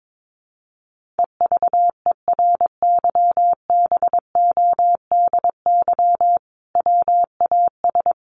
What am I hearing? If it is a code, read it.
Morse code, heard as a keyed tone, about 22 wpm, 700 Hz. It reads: EVERYBODY WAS
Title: Sentences from Top 1400 Words 22wpm